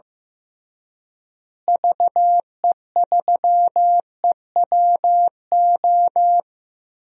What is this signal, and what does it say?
Morse code, heard as a keyed tone, about 15 wpm, 700 Hz. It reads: VE3EWO
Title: DXCC-100 Call Signs - 6 Characters - Repeated 3x - 15wpm